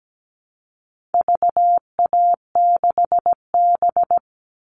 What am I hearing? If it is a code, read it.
Morse code, heard as a keyed tone, about 17 wpm, 700 Hz. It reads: VA6B
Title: All Cut Numbers - 4-Digit 17wpm